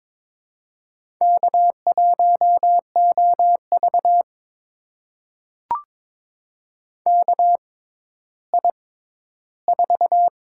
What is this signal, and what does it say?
Morse code, heard as a keyed tone, about 22 wpm, spaced as for 13 wpm, 700 Hz. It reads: K1OV KI4
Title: USA Call Signs - 6x Character Spacing - 22wpm